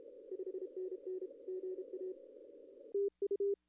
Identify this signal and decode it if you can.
Morse code, heard as a keyed tone, about 30 wpm, 400 Hz. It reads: 5NN GA TU
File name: contests / arrl-160-k5tr-op-w5tn -0800